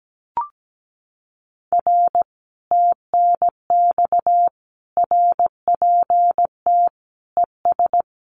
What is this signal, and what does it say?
Morse code, heard as a keyed tone, about 17 wpm, 700 Hz. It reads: R TNX RPT ES
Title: Common QSO Phrases Encore 17wpm